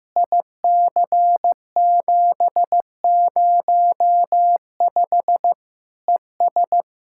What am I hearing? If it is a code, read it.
Morse code, heard as a keyed tone, about 15 wpm, 700 Hz. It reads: IC705 ES
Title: Practice QSO - Exchange 2 - Info - 15wpm